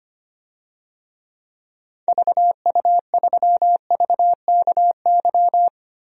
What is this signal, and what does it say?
Morse code, heard as a keyed tone, about 25 wpm, 700 Hz. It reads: VU3VKY
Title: DXCC-100 Call Signs - 6 Characters - Repeated 3x - 25wpm